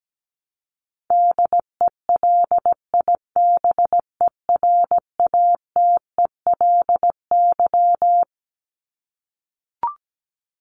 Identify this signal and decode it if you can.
Morse code, heard as a keyed tone, about 17 wpm, 700 Hz. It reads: DELIBERATELY
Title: Top 1751-2000 Words 17wpm